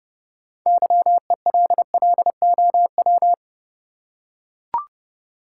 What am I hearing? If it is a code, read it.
Morse code, heard as a keyed tone, about 30 wpm, 700 Hz. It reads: YELLOW
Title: Six Letter Words - 30wpm